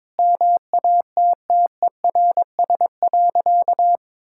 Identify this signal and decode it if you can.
Morse code, heard as a keyed tone, about 22 wpm, 700 Hz. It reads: MATTERS.
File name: Sentences with 9-Word Phrases - 22wpm